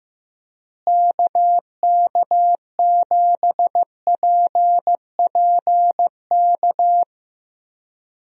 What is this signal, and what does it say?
Morse code, heard as a keyed tone, about 16 wpm, 700 Hz. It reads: KK7PPK
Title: USA Call Signs - 6 Characters - Repeated 2x - 15wpm